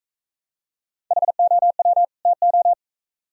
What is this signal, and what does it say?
Morse code, heard as a keyed tone, about 39 wpm, 700 Hz. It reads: HOW TO